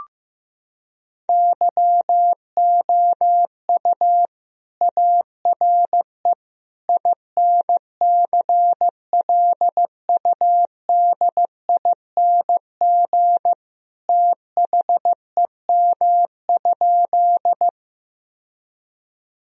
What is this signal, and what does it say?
Morse code, heard as a keyed tone, about 16 wpm, 700 Hz. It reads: YOU ARE INCLUDING THEM?
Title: Sentences from Top 1000 Words - Review Part 4 - 15wpm